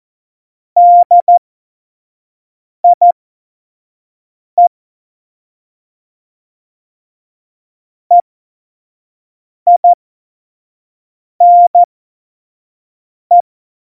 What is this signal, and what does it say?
Morse code, heard as a keyed tone, about 14 wpm, spaced as for 5 wpm, 700 Hz. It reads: DIE EINE
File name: Morsekurs / Witze / Witz 17.Juli 2018, 25 F70